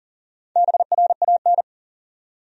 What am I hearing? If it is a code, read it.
Morse code, heard as a keyed tone, about 40 wpm, 700 Hz. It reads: BRAN
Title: USA Common Names - 40wpm